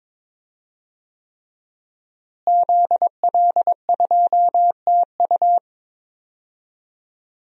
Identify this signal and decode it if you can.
Morse code, heard as a keyed tone, about 21 wpm, 700 Hz. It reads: ZL2TU
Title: DXCC-100 Call Signs - 5 Characters - Repeated 3x - 22wpm